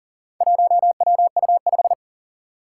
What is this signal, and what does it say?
Morse code, heard as a keyed tone, about 44 wpm, 690 Hz. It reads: 1WU5
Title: Four Challenging Letter-Number Combinations 40wpm